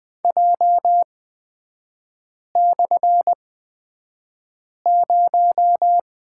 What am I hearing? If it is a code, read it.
Morse code, heard as a keyed tone, about 20 wpm, spaced as for 5 wpm, 700 Hz. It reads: J/0
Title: 224 Lesson 36e - BT - Three Characters 20wpm